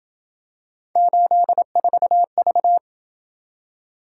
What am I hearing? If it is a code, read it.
Morse code, heard as a keyed tone, about 27 wpm, 700 Hz. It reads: 84V